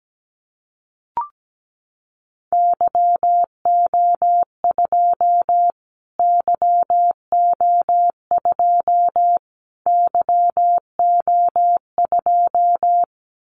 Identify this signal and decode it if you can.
Morse code, heard as a keyed tone, about 17 wpm, 700 Hz. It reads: YO2 YO2 YO2
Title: Callsign Prefixes for Top 25 DXCC - Repeated 3x - 17wpm